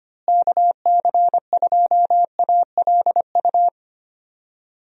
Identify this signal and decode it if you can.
Morse code, heard as a keyed tone, about 25 wpm, 700 Hz. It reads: KC2ALU
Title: USA Call Signs - 6 Characters - Repeated 2x - 25wpm